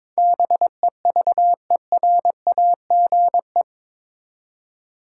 Morse code, heard as a keyed tone, about 22 wpm, 700 Hz. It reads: BEVERAGE
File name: practice / 22wpm / Eight Letter Words - 22wpm